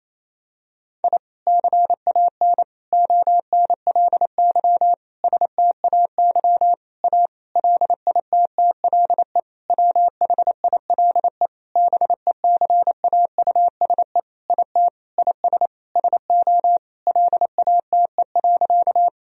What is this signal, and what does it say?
Morse code, heard as a keyed tone, about 28 wpm, 700 Hz. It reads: I CAN ONLY STAY A LITTLE WHILE BECAUSE IT IS SO LATE.
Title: Sentences from Top 1000 Words - Review Part 3 - 28wpm